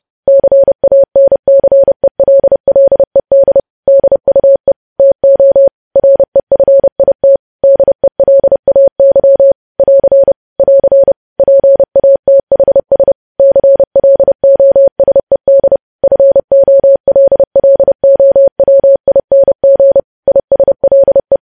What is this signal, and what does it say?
Morse code, heard as a keyed tone, about 30 wpm, 550 Hz. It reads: CANCELLED DUE TO REFIT DELAY <AR> <AR> PATHS CLOSED FOLLOWING ISLE
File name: podcast / News headlines-1x-30x30